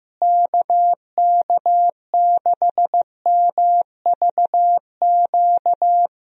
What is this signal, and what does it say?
Morse code, heard as a keyed tone, about 15 wpm, 700 Hz. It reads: KK6MVQ
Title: USA Call Signs - 6 Characters - Repeated 3x - 15wpm